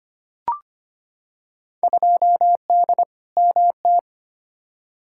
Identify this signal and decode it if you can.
Morse code, heard as a keyed tone, about 25 wpm, 700 Hz. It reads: 2D MT
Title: Field Day Exchanges Encore - 25wpm